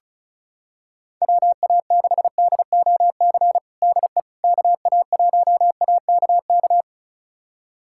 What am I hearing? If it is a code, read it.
Morse code, heard as a keyed tone, about 34 wpm, 690 Hz. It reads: WA6DOC DE KA1AKK
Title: USA Callsign DE Callsign - 6 Characters - 35wpm